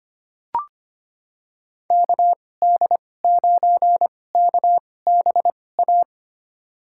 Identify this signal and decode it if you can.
Morse code, heard as a keyed tone, about 25 wpm, spaced as for 18 wpm, 700 Hz. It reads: KD9KBA
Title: USA Call Signs - 6 Characters - 2x Character Spacing - 25wpm